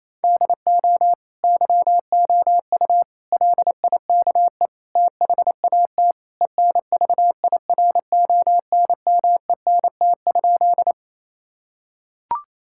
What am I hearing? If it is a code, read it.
Morse code, heard as a keyed tone, about 28 wpm, 700 Hz. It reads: DO YOU LIKE THAT ENVIRONMENT?
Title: Sentences from Top 1000 Words - Review Part 5 - 28wpm